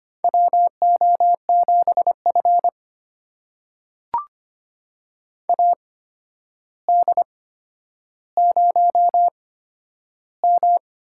Morse code, heard as a keyed tone, about 25 wpm, spaced as for 11 wpm, 700 Hz. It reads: WO7F AD0M
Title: USA Call Signs - 4 Characters - 8x Character Spacing - 25wpm